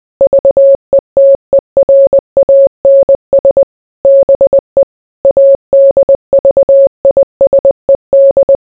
Morse code, heard as a keyed tone, about 20 wpm, 550 Hz. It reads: VETERANS BE ADVISED